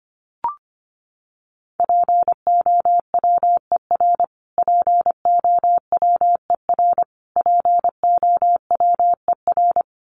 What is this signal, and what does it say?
Morse code, heard as a keyed tone, about 25 wpm, 700 Hz. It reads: POWER POWER POWER
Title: Top 500 Words Repeated 25wpm